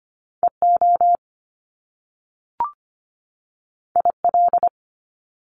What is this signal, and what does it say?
Morse code, heard as a keyed tone, about 25 wpm, 700 Hz. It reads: EO IL
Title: Top Two Letter Combinations in Words 25wpm